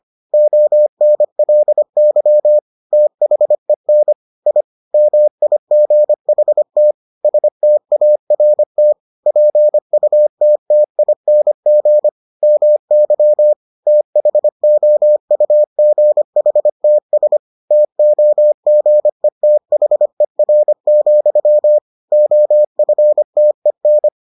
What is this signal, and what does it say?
Morse code, heard as a keyed tone, about 25 wpm, 600 Hz. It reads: ONLY THEN I MIGHT START PUTTING MY THOUGHTS TOGETHER, OFTEN